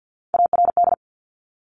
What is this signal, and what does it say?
Morse code, heard as a keyed tone, about 50 wpm, 700 Hz. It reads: ARD
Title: Top Three Letter Combinations in Words 50wpm